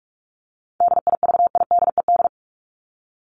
Morse code, heard as a keyed tone, about 44 wpm, 700 Hz. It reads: DIVIDED
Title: ED Suffix - Top 100 Words with Base Words 45wpm